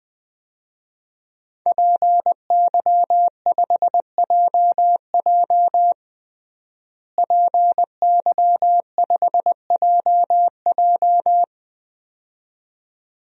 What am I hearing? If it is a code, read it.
Morse code, heard as a keyed tone, about 21 wpm, 700 Hz. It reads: PY5JJ PY5JJ
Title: DXCC-100 Call Signs - 5 Characters - Repeated 2x - 20wpm